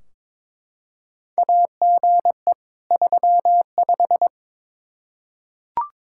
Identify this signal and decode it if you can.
Morse code, heard as a keyed tone, about 21 wpm, 700 Hz. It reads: AGE 35
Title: Practice QSO - Exchange 3 - More Info - 22wpm